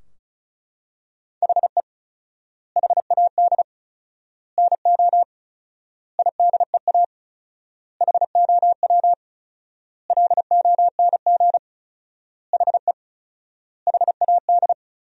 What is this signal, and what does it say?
Morse code, heard as a keyed tone, about 36 wpm, 700 Hz. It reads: HE HAD NO IDEA HOW LONG HE HAD
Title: 4x Word-Spacing - Sentences from Top 300 Words - 35wpm